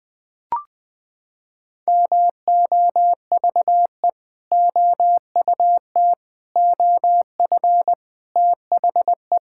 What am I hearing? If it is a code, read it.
Morse code, heard as a keyed tone, about 21 wpm, 700 Hz. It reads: MOVE OUT OF THE
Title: Sentences with 4-Word Phrases - 20wpm